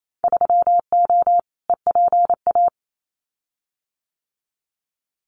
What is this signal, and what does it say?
Morse code, heard as a keyed tone, about 28 wpm, 700 Hz. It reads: 3O EPA
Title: Winter Field Day Exchanges 28wpm